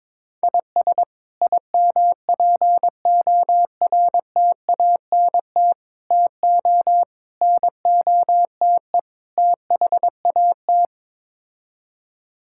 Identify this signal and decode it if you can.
Morse code, heard as a keyed tone, about 22 wpm, 700 Hz. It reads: IS IMPORTANT TO NOTE THAT
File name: practice / 22wpm / Build A Phrase - 5-Word Phrases - 22wpm